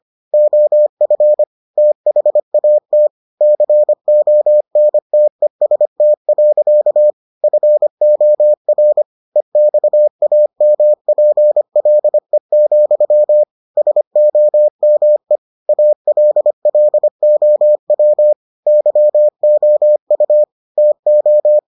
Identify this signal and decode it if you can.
Morse code, heard as a keyed tone, about 25 wpm, 600 Hz. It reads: OF THAT CONTEST. FOR EXAMPLE, SOME ALLOW YOU TO